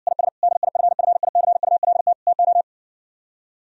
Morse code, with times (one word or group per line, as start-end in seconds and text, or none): 0.1-0.3 IS
0.4-2.1 DIFFICULT
2.3-2.6 TO